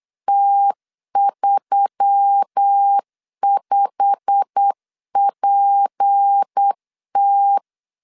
Morse code, heard as a keyed tone, about 8 wpm, 800 Hz. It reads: T35PT
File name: CW / DET / 1 9 1500C S B 8wpm 5624